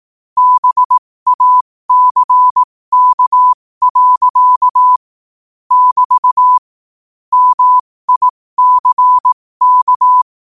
Morse code, heard as a keyed tone, about 18 wpm, spaced as for 15 wpm, 1000 Hz. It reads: BACK. <BT> MICK
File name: qotdfiles / 15wpm